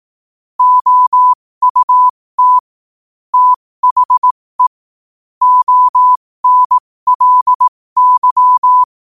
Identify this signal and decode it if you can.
Morse code, heard as a keyed tone, about 18 wpm, spaced as for 15 wpm, 1000 Hz. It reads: OUT THE ONLY
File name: qotdfiles / 15wpm